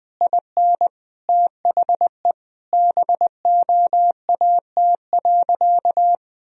Morse code, heard as a keyed tone, about 20 wpm, 700 Hz. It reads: IN THE BOAT.
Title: Sentences from Top 1000 Words - Review Part 3 - 20wpm